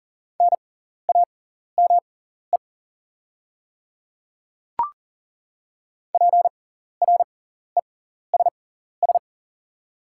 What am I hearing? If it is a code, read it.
Morse code, heard as a keyed tone, about 40 wpm, spaced as for 14 wpm, 690 Hz. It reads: NAME PRESS
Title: 6x Character Spacing - Top 500 Words - 40wpm